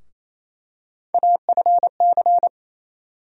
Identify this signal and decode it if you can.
Morse code, heard as a keyed tone, about 28 wpm, 700 Hz. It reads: AFC
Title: USA Call Sign Postfixes - 3 Letters 28wpm